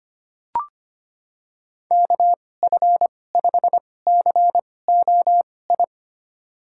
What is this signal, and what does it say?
Morse code, heard as a keyed tone, about 25 wpm, spaced as for 18 wpm, 700 Hz. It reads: KF5COI
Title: USA Call Signs - 6 Characters - 2x Character Spacing - 25wpm